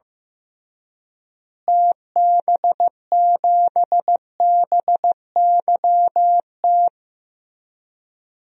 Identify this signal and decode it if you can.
Morse code, heard as a keyed tone, about 16 wpm, 700 Hz. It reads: TB7BYT
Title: DXCC-100 Call Signs - 6 Characters - Repeated 3x - 15wpm